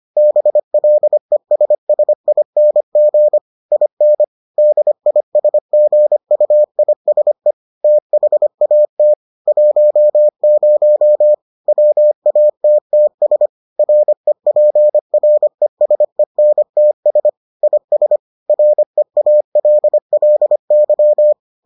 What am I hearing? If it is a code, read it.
Morse code, heard as a keyed tone, about 25 wpm, 600 Hz. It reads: BLESSING IN DISGUISE THAT 10 WATTS REPRESENTS IS REALLY